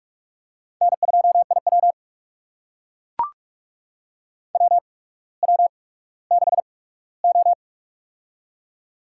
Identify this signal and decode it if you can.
Morse code, heard as a keyed tone, about 44 wpm, spaced as for 19 wpm, 700 Hz. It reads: N2IW WW6O
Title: USA Call Signs - 4 Characters - 8x Character Spacing - 45wpm